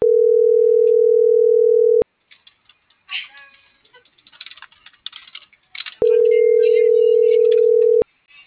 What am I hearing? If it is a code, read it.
Ringback tone (North America), 440 + 480 Hz.
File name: CENTRAL MABE / out xquenda.trinidad